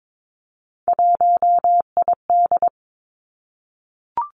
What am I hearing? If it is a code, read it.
Morse code, heard as a keyed tone, about 22 wpm, 700 Hz. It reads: 1ID
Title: Three Challenging Letter-Number Combinations 22wpm